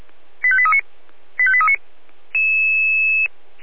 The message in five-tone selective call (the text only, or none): 85839 85839